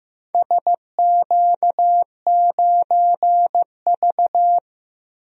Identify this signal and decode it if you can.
Morse code, heard as a keyed tone, about 15 wpm, 700 Hz. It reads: SQ9V
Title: DXCC-100 Call Signs - 4 Characters - Repeated 2x - 15wpm